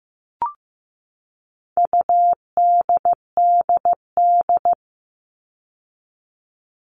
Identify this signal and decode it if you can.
Morse code, heard as a keyed tone, about 15 wpm, 700 Hz. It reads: UDDD
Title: All Cut Numbers - 4-Digit 15wpm